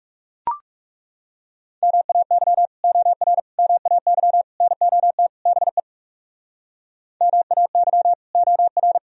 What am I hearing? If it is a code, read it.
Morse code, heard as a keyed tone, about 37 wpm, 700 Hz. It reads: MAY OR MAY NOT BE MAY OR
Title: Speed-Racing - Sets of 5 Words 30wpm